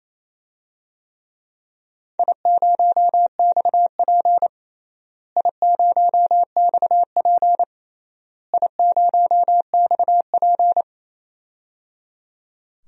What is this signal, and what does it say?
Morse code, heard as a keyed tone, about 28 wpm, 700 Hz. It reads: I0XP I0XP I0XP
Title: DXCC-100 Call Signs - 4 Characters - Repeated 3x - 28wpm